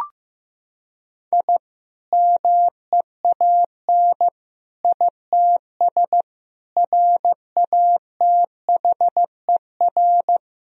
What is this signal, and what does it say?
Morse code, heard as a keyed tone, about 15 wpm, 700 Hz. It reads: I MEAN ITS RATHER
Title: Sentences from Top 1000 Words - Review Part 4 - 15wpm